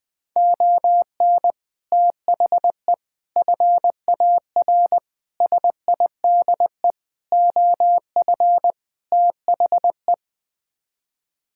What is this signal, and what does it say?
Morse code, heard as a keyed tone, about 20 wpm, 700 Hz. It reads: ON THE FAR SIDE OF THE
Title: Sets of 6 Words - 20wpm